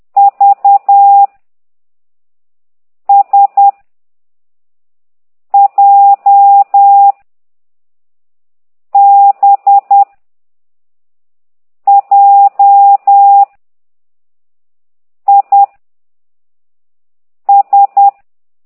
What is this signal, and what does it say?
Morse code, heard as a keyed tone, about 10 wpm, 800 Hz. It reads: V S J B J I S
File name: exerc7